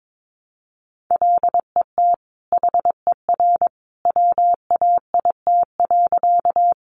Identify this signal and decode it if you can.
Morse code, heard as a keyed tone, about 22 wpm, 700 Hz. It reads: LET HER WAIT.
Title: Sentences from Top 500 Words 22wpm